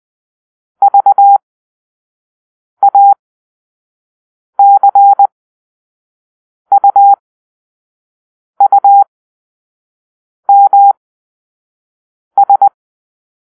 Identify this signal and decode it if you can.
Morse code, heard as a keyed tone, about 20 wpm, spaced as for 5 wpm, 800 Hz. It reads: VACUUMS
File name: arrl / Grp-5 Drill-4